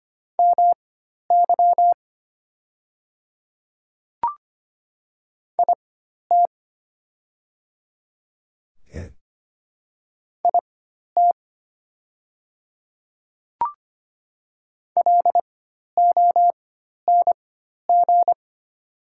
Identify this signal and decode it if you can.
Morse code, heard as a keyed tone, about 25 wpm, spaced as for 12 wpm, 700 Hz. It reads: MY IT IT LONG
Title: 4x Character Spacing - Top 100 Words - 25wpm